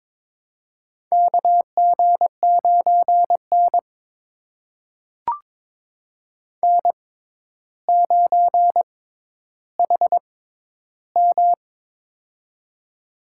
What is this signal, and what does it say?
Morse code, heard as a keyed tone, about 22 wpm, spaced as for 11 wpm, 700 Hz. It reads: KG9N N9HM
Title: USA Call Signs - 6x Character Spacing - 22wpm